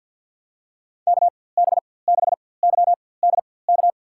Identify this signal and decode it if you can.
Morse code, heard as a keyed tone, about 53 wpm, spaced as for 24 wpm, 700 Hz. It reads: KB6YDK
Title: USA Call Signs - 6 Characters - 4x Character Spacing - 50wpm